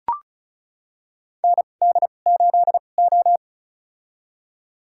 Morse code, heard as a keyed tone, about 36 wpm, spaced as for 25 wpm, 700 Hz. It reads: ND8O
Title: USA Call Signs - 2x Character Spacing - 35wpm